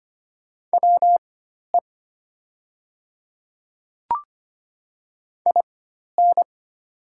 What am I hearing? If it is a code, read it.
Morse code, heard as a keyed tone, about 25 wpm, spaced as for 12 wpm, 700 Hz. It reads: WE IN
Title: 4x Character Spacing - Top 100 Words - 25wpm